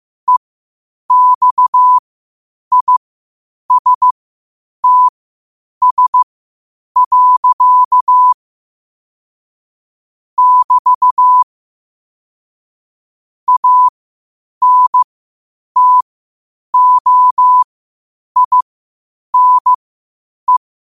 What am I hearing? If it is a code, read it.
Morse code, heard as a keyed tone, about 15 wpm, spaced as for 8 wpm, 1000 Hz. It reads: EXISTS. <BT> ANTOINE
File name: qotdfiles / 8wpm